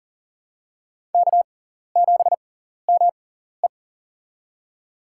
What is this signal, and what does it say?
Morse code, heard as a keyed tone, about 38 wpm, spaced as for 14 wpm, 700 Hz. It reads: K7ME
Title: USA Call Signs - 4 Characters - 6x Character Spacing - 40wpm